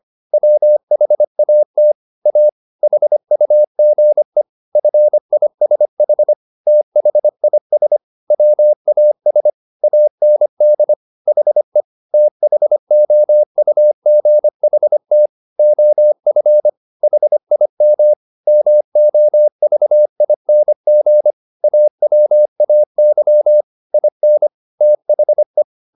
Morse code, heard as a keyed tone, about 25 wpm, 600 Hz. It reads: WHAT A HUGE FISH THIS WAS AND HE THOUGHT OF HIM MOVING AWAY IN THE
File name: the old man and the sea part 1 25wpm